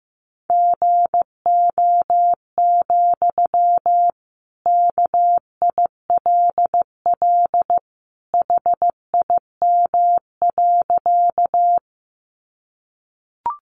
Morse code, heard as a keyed tone, about 15 wpm, 700 Hz. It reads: GO, KILL HIM.
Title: Sentences from Top 1400 Words 15wpm